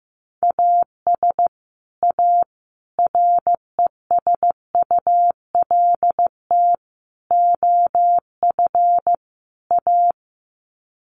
Morse code, heard as a keyed tone, about 15 wpm, 700 Hz. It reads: AS A RESULT OF A
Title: Sets of 5 Words Encore 15wpm